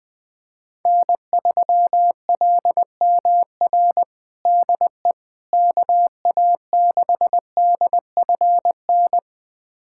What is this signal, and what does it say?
Morse code, heard as a keyed tone, about 20 wpm, 700 Hz. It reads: N3LMR DE KA6DFN
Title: USA Callsign DE Callsign 20wpm